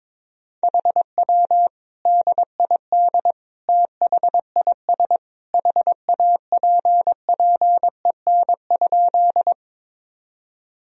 Morse code, heard as a keyed tone, about 21 wpm, 700 Hz. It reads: HW DID THIS HAPPEN?
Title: Sentences from Top 400 Words and Abbreviations 22wpm